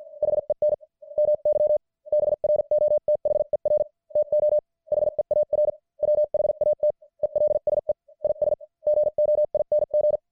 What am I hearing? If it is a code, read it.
Morse code, heard as a keyed tone, about 50 wpm, spaced as for 44 wpm, 600 Hz. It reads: HEN MY BROTHER TO HEAR WHAT ELSE IS GOING
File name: archived / CWBros